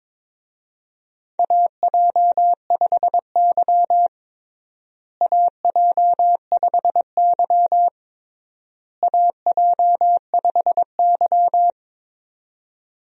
Morse code, heard as a keyed tone, about 22 wpm, 700 Hz. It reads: AJ5Y AJ5Y AJ5Y
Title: USA Call Signs - 4 Characters - Repeated 3x - 22wpm